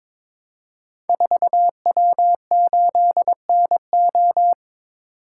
Morse code, heard as a keyed tone, about 22 wpm, 700 Hz. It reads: 4W8NO
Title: Code Groups - 5 Letter:Numbers - 22wpm